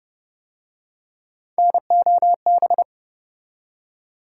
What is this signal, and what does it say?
Morse code, heard as a keyed tone, about 33 wpm, 700 Hz. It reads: NOB